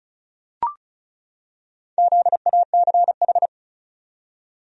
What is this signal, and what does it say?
Morse code, heard as a keyed tone, about 36 wpm, 700 Hz. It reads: ZACH
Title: USA Common Names - Max 4 Letters - 35wpm